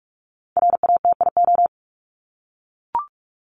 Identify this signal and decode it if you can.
Morse code, heard as a keyed tone, about 44 wpm, 700 Hz. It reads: RATIO
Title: Top 1751-2000 Words 45wpm